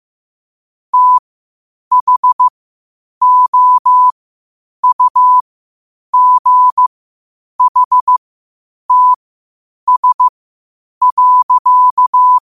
Morse code, heard as a keyed tone, about 15 wpm, spaced as for 8 wpm, 1000 Hz. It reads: THOUGHTS.
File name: qotdfiles / 8wpm